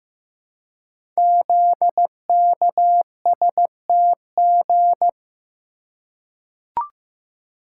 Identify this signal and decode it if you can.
Morse code, heard as a keyed tone, about 15 wpm, 700 Hz. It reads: ZKSTG
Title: Code Groups - 5 Letters - 15wpm